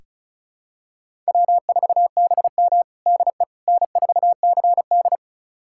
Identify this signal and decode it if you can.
Morse code, heard as a keyed tone, about 36 wpm, 700 Hz. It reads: W4BM DE N4CD
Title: USA Callsign DE Callsign - 4 Characters - 35wpm